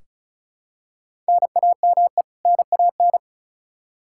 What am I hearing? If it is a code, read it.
Morse code, heard as a keyed tone, about 36 wpm, 700 Hz. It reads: NAME NAN
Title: Practice QSO - Exchange 1 - Beginning - 35wpm